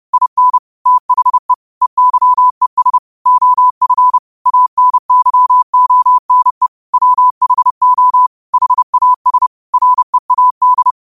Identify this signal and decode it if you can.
Morse code, heard as a keyed tone, about 30 wpm, 1000 Hz. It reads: IN THE EYES OF ANYONE WHO HAS READ
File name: qotdfiles / 30wpm